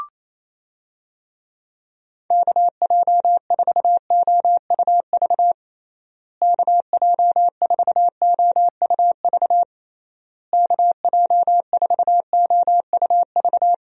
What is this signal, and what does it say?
Morse code, heard as a keyed tone, about 28 wpm, 700 Hz. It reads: KJ4OUV KJ4OUV KJ4OUV
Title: USA Call Signs - 6 Characters - Repeated 3x - 28wpm